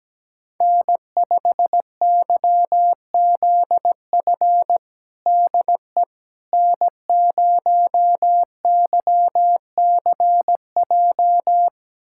Morse code, heard as a keyed tone, about 17 wpm, 700 Hz. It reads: N5YZF DE N0YCJ
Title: USA Callsign DE Callsign - 5 Characters - 17wpm